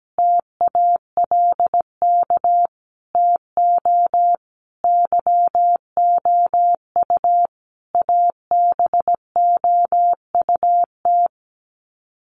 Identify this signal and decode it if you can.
Morse code, heard as a keyed tone, about 17 wpm, 700 Hz. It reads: TALK TO YOU ABOUT
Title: Sets of 5 Words Encore 17wpm